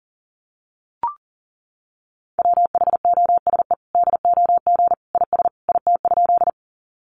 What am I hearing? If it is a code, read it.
Morse code, heard as a keyed tone, about 37 wpm, 700 Hz. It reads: WHOSE DOG IS IT?
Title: Sentences from Top 1200 Words 40wpm